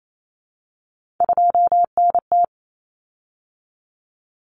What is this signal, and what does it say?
Morse code, heard as a keyed tone, about 28 wpm, 700 Hz. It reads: 2NT
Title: Common Cut Numbers - 3-Digit 28wpm